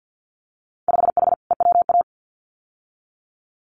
Morse code, heard as a keyed tone, about 50 wpm, 700 Hz. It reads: 5H EWA
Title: Winter Field Day Exchanges Encore 50wpm